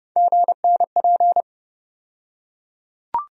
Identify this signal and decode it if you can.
Morse code, heard as a keyed tone, about 30 wpm, 700 Hz. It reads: GNP
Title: USA Call Sign Postfixes - 3 Letters 30wpm